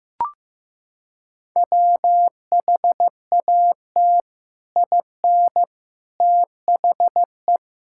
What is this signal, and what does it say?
Morse code, heard as a keyed tone, about 15 wpm, 700 Hz. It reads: WHAT IN THE
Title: Sentences from Top 1000 Words - Review Part 6 - 15wpm